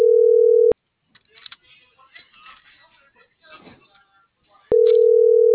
Ringback tone (North America), 440 + 480 Hz.
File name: CENTRAL SAGOT / out diego.emir - .47578